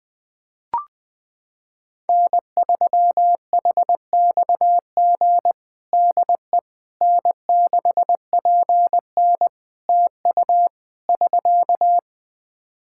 Morse code, heard as a keyed tone, about 20 wpm, 700 Hz. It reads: N3HXG DE N6PN TU <SK>
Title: K1USN SST QSO Practice 20wpm